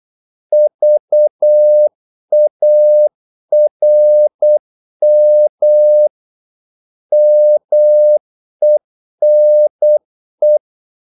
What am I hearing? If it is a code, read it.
Morse code, heard as a keyed tone, about 8 wpm, 600 Hz. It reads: VARM MENE
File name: Korte ord DK 8wpm